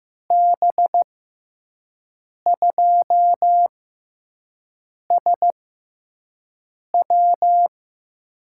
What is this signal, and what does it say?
Morse code, heard as a keyed tone, about 15 wpm, 700 Hz. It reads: B 2 S W
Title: USA Call Signs - 5 Characters - 6x Character Spacing - 15wpm